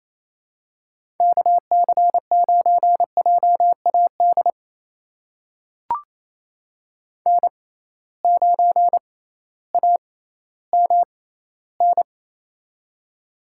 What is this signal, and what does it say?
Morse code, heard as a keyed tone, about 28 wpm, spaced as for 15 wpm, 700 Hz. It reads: KC9JAD N9AMN
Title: USA Call Signs - 6x Character Spacing - 28wpm